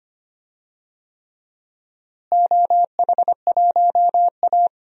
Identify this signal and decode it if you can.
Morse code, heard as a keyed tone, about 25 wpm, 700 Hz. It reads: OH1A
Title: DXCC-100 Call Signs - 4 Characters - Repeated 3x - 25wpm